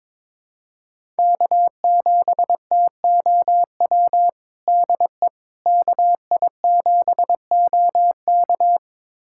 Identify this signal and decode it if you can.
Morse code, heard as a keyed tone, about 22 wpm, 700 Hz. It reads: K7TOW DE KI7OK
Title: USA Callsign DE Callsign - 5 Characters - 22wpm